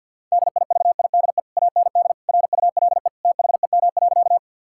Morse code, heard as a keyed tone, about 51 wpm, 700 Hz. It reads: DIVIDE AND RULE THEM.
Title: Sentences from Top 1600 Words 50wpm